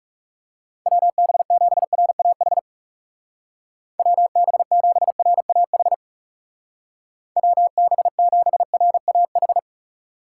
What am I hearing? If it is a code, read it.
Morse code, heard as a keyed tone, about 36 wpm, 700 Hz. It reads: WB7RAH WB7RAH WB7RAH
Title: Speed Racing - USA 2x3 Call Signs 30wpm